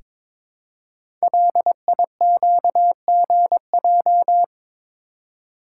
Morse code, heard as a keyed tone, about 22 wpm, 700 Hz. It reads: LIQGJ
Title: Code Groups - 5 Letters - 22wpm